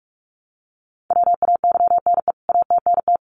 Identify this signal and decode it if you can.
Morse code, heard as a keyed tone, about 44 wpm, 700 Hz. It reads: WAYNE ATNT
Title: CWT Exchange 45wpm